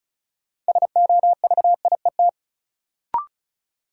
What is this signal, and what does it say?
Morse code, heard as a keyed tone, about 36 wpm, 700 Hz. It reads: SOVIET
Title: Sets of 5 Words Encore - New Word Drill - 35wpm